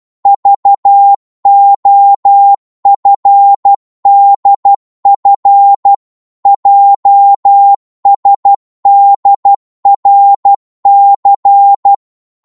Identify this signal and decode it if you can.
Morse code, heard as a keyed tone, about 12 wpm, 800 Hz. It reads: VOFDF JSDRC
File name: Morse / ltr-12wpm